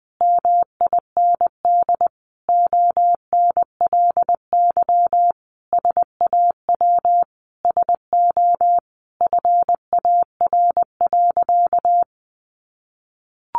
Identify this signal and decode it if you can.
Morse code, heard as a keyed tone, about 20 wpm, 700 Hz. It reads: MIND ONLY SAW SO FAR.
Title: Sentences from Top 800 Words 20wpm